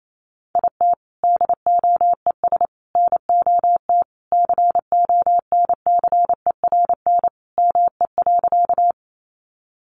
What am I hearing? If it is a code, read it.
Morse code, heard as a keyed tone, about 28 wpm, 700 Hz. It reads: IT DOES NOT CONCERN ME.
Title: Sentences from Top 1000 Words 28wpm